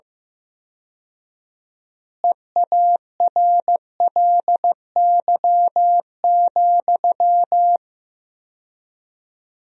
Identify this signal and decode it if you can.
Morse code, heard as a keyed tone, about 16 wpm, 700 Hz. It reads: EARLY,
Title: 4x Word-Spacing - Sentences from Top 400 Words - 15wpm